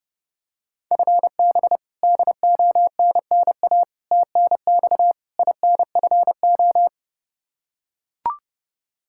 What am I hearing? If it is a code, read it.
Morse code, heard as a keyed tone, about 30 wpm, 700 Hz. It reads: FB DONNA TNX INFO
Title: Practice QSO - Exchange 3 - More Info - 30wpm